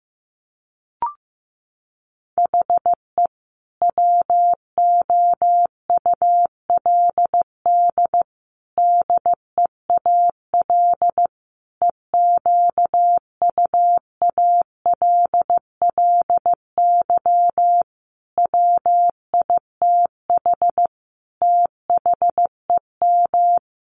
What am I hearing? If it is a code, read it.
Morse code, heard as a keyed tone, about 15 wpm, 700 Hz. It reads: HE WOULD DEAL EQUALLY WITH THEM
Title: Sentences from Top 1400 Words 15wpm